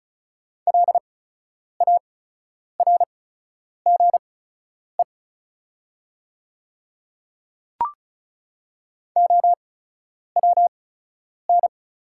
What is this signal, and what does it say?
Morse code, heard as a keyed tone, about 36 wpm, spaced as for 10 wpm, 700 Hz. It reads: LARGE OWN
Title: 8x Character Spacing - Top 300 Words - 35wpm